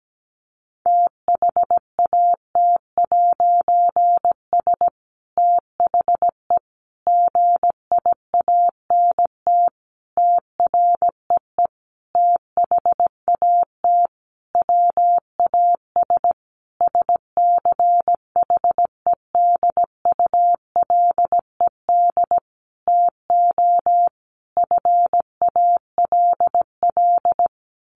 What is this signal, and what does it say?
Morse code, heard as a keyed tone, about 17 wpm, 700 Hz. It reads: THAT'S THE GIANT TREE THAT WAS SCHEDULED TO FALL
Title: A Sound Of Thunder 17wpm